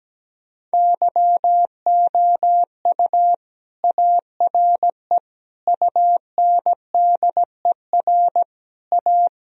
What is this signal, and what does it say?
Morse code, heard as a keyed tone, about 17 wpm, 700 Hz. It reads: YOU ARE UNDER A
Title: Sentences from Top 1000 Words - Review Part 1 - 17wpm